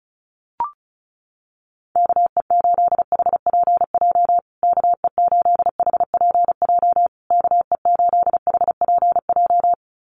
Morse code, heard as a keyed tone, about 34 wpm, 700 Hz. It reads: KE8HPJ KE8HPJ KE8HPJ
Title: USA Call Signs Repeated 3x 35wpm